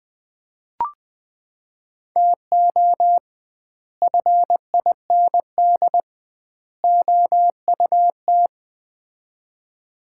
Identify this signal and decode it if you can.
Morse code, heard as a keyed tone, about 20 wpm, 700 Hz. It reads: TO FIND OUT
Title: 2x Word-Spacing - Sets of 3 Words - 20wpm